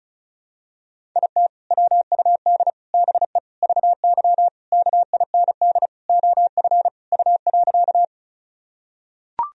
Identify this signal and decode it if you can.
Morse code, heard as a keyed tone, about 36 wpm, 700 Hz. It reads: IT WUD BE VY KIND OF U.
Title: Sentences from Top 200 Words and Abbreviations 35wpm